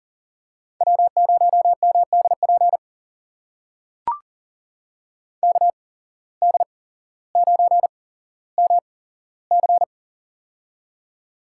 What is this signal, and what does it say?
Morse code, heard as a keyed tone, about 40 wpm, spaced as for 17 wpm, 700 Hz. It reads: W0MDP KD9MC
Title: USA Call Signs - 5 Characters - 8x Character Spacing - 40wpm